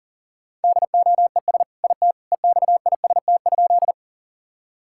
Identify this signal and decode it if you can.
Morse code, heard as a keyed tone, about 40 wpm, 700 Hz. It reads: DOES IT EXIST?
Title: Sentences from Top 1000 Words - Review Part 5 - 40wpm